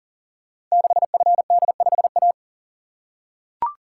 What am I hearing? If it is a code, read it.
Morse code, heard as a keyed tone, about 39 wpm, 700 Hz. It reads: 6FD5A
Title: Code Groups - 5 Letter:Numbers - 40wpm